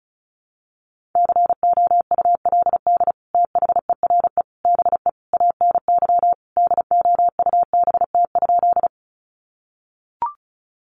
Morse code, heard as a keyed tone, about 34 wpm, 700 Hz. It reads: COULD THERE BE ANY DOUBT?
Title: Sentences from Top 1500 Words 35wpm